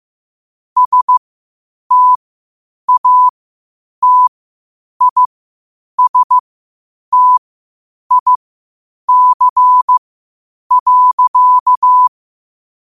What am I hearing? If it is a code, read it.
Morse code, heard as a keyed tone, about 15 wpm, spaced as for 8 wpm, 1000 Hz. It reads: STATISTIC.
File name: qotdfiles / 8wpm